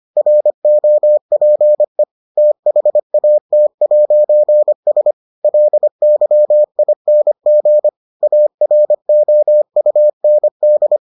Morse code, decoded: ROPE THAT'S LYING AROUND

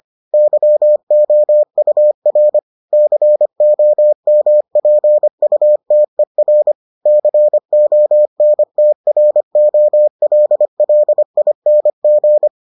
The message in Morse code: YOUR COMPUTER CONTROLLING

